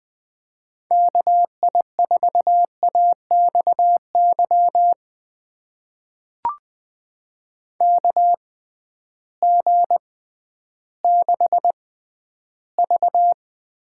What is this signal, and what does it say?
Morse code, heard as a keyed tone, about 20 wpm, spaced as for 12 wpm, 700 Hz. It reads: KI4AXY KG6V
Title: USA Call Signs - 6 Characters - 6x Character Spacing - 20wpm